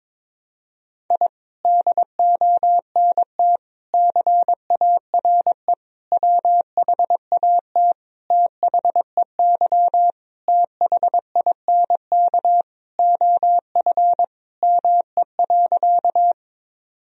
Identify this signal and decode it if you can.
Morse code, heard as a keyed tone, about 22 wpm, 700 Hz. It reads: I DONT CARE WHAT THEY THINK OF ME.
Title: Sentences from Top 1000 Words - Review Part 1 - 22wpm